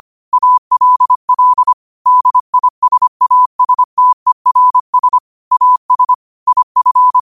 Morse code, heard as a keyed tone, about 25 wpm, 1000 Hz. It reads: ALL DISASTERS AS IF
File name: 25wpm